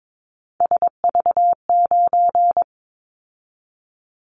Morse code, heard as a keyed tone, about 22 wpm, 700 Hz. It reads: SV9
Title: Callsign Prefixes for Top 100 DXCC - 22wpm